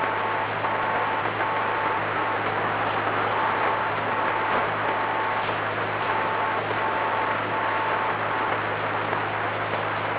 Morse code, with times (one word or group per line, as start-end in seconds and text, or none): none